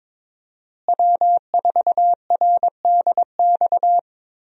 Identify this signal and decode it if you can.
Morse code, heard as a keyed tone, about 22 wpm, 700 Hz. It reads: W4RDX
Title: USA Call Signs - 5 Characters - Repeated 2x - 22wpm